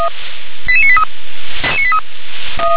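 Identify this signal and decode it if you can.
Five-tone selective call (ZVEI), heard as quick runs of five tones, tones of about 70 ms. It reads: 88983
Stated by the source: ZVEI five-tone selective call received on an SDR.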